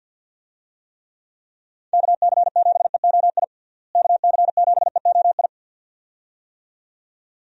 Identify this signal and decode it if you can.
Morse code, heard as a keyed tone, about 52 wpm, 700 Hz. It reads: KC7EOI KC7EOI
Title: USA Call Signs - 6 Characters - Repeated 2x - 50wpm